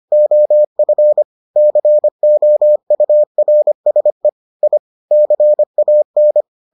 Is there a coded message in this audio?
Yes: Morse code from 0.1 to 6.4 s, heard as a keyed tone, about 25 wpm, 600 Hz. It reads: OF COURSE I CAN